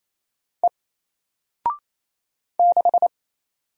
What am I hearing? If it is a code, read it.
Morse code, heard as a keyed tone, about 28 wpm, 700 Hz. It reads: E 6
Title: Single Letter-Number 28wpm